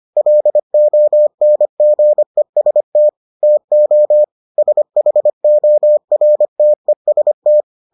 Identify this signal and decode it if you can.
Morse code, heard as a keyed tone, about 25 wpm, 600 Hz. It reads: LONGEST TO SHORTEST